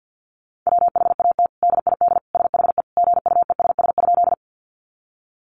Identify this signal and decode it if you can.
Morse code, heard as a keyed tone, about 50 wpm, 700 Hz. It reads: WHAT DID SHE GUESS?
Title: Sentences from Top 1500 Words 50wpm